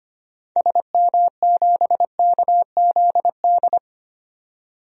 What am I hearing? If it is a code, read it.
Morse code, heard as a keyed tone, about 25 wpm, 700 Hz. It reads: SM7KZD
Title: DXCC-100 Call Signs - 6 Characters - 25wpm